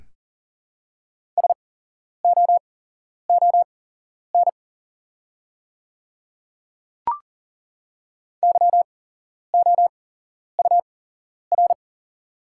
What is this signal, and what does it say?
Morse code, heard as a keyed tone, about 41 wpm, spaced as for 11 wpm, 700 Hz. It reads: SOON YOUR
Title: 8x Character Spacing - Top 500 Words - 40wpm